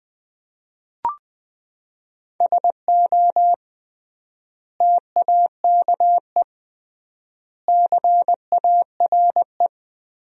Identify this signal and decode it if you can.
Morse code, heard as a keyed tone, about 19 wpm, 700 Hz. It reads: SO TAKE CARE
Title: 3x Word-Spacing - Sentences from Top 300 Words - 20wpm